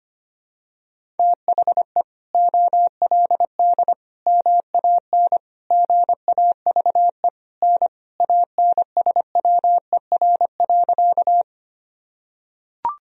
Morse code, heard as a keyed tone, about 25 wpm, 700 Hz. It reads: THE OLD MAN GAVE N ANSWER.
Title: Sentences from Top 500 Words and Abbreviations 25wpm